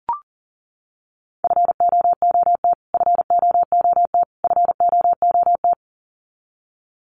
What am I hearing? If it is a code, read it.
Morse code, heard as a keyed tone, about 40 wpm, 700 Hz. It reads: FOOT FOOT FOOT
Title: Top 500 Words Repeated 40wpm